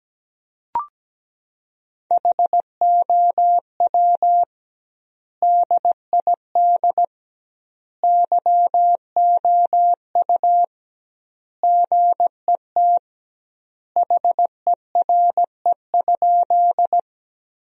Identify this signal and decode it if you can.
Morse code, heard as a keyed tone, about 17 wpm, 700 Hz. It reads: HOW DID YOU GET HERE?
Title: 2x Word-Spacing - Sentences from Top 200 Words - 17wpm